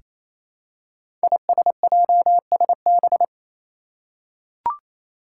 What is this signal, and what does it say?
Morse code, heard as a keyed tone, about 28 wpm, 700 Hz. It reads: ISJSB
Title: Five Challenging Letter-Number Combinations 28wpm